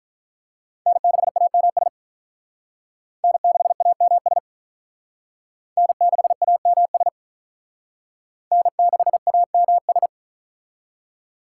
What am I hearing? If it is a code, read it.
Morse code, heard as a keyed tone, about 45 wpm, 700 Hz. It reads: N6AMS N6AMS N6AMS N6AMS